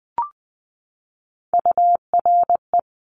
Morse code, heard as a keyed tone, about 20 wpm, 700 Hz. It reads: URE
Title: Top Three Letter Combinations in Words 20wpm